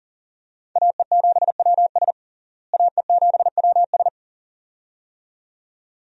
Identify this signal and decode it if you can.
Morse code, heard as a keyed tone, about 40 wpm, 700 Hz. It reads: AE7WS AE7WS
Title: USA Call Signs - 5 Characters - Repeated 2x - 40wpm